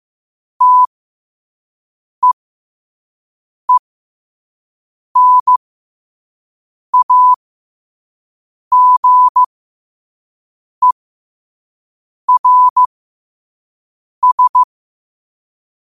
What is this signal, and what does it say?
Morse code, heard as a keyed tone, about 15 wpm, spaced as for 5 wpm, 1000 Hz. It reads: TEENAGERS
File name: qotdfiles / 5wpm